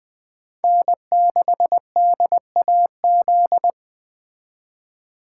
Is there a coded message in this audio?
Yes: Morse code from 0.6 to 3.7 s, heard as a keyed tone, about 21 wpm, 700 Hz. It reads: N6DAZ